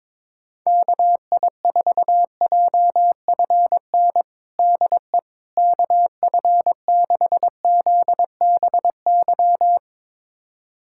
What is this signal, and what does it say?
Morse code, heard as a keyed tone, about 22 wpm, 700 Hz. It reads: KI4JFN DE KF6ZBY
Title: USA Callsign DE Callsign - 6 Characters - 22wpm